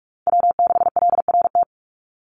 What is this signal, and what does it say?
Morse code, heard as a keyed tone, about 44 wpm, 700 Hz. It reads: W6LRT
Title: USA Call Signs 45wpm